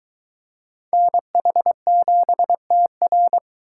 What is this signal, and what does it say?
Morse code, heard as a keyed tone, about 23 wpm, 700 Hz. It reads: NH7TR